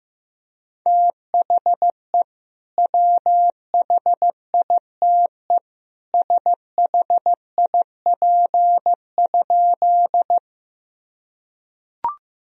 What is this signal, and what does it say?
Morse code, heard as a keyed tone, about 15 wpm, 700 Hz. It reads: THE WHITE SHIP?
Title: Sentences from Top 1000 Words - Review Part 2 - 15wpm